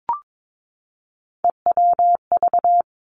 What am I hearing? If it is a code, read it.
Morse code, heard as a keyed tone, about 22 wpm, 700 Hz. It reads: EWV
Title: Three Challenging Letter-Number Combinations 22wpm